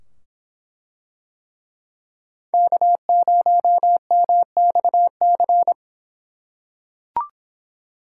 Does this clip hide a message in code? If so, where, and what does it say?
Yes: Morse code from 2.5 to 5.7 s, heard as a keyed tone, about 24 wpm, 700 Hz. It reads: K0MXC